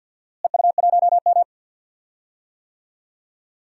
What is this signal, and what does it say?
Morse code, heard as a keyed tone, about 52 wpm, 700 Hz. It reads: EU1M
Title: DXCC-100 Call Signs - 4 Characters - Repeated 2x - 50wpm